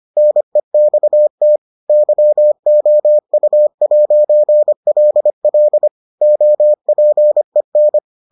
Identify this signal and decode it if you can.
Morse code, heard as a keyed tone, about 25 wpm, 600 Hz. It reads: NEXT YOU'LL OPEN